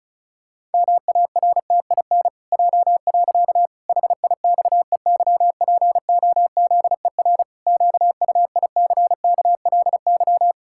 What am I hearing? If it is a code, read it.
Morse code, heard as a keyed tone, about 36 wpm, 700 Hz. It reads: MARTIN J. HIXEYPOZER QUICKLY
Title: Pangram Challenge v2 - 35wpm